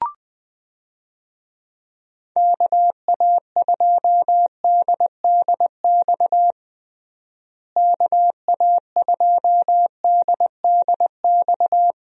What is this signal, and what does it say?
Morse code, heard as a keyed tone, about 20 wpm, 700 Hz. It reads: KA2DDX KA2DDX
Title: USA Call Signs - 6 Characters - Repeated 2x - 20wpm